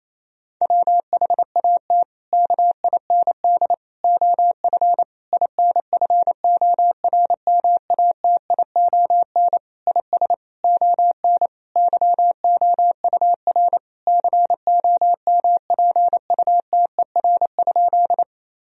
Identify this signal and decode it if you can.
Morse code, heard as a keyed tone, about 27 wpm, 700 Hz. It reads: WHAT KIND OF INFORMATION IS ON YOUR COMPUTER?
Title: Sentences from Top 1000 Words - Review Part 3 - 28wpm